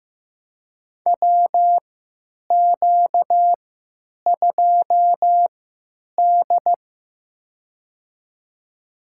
Morse code, heard as a keyed tone, about 15 wpm, spaced as for 9 wpm, 700 Hz. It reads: WQ2D
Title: USA Call Signs - 4 Characters - 3x Character Spacing - 15wpm